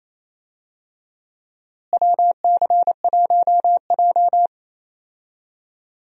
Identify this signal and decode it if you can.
Morse code, heard as a keyed tone, about 28 wpm, 700 Hz. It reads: WC1J
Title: USA Call Signs - 4 Characters - Repeated 3x - 28wpm